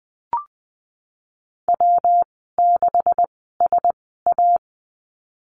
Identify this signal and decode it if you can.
Morse code, heard as a keyed tone, about 20 wpm, spaced as for 14 wpm, 700 Hz. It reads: W6SA
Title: 147 Lesson 24g - Letter y - Call Signs 20wpm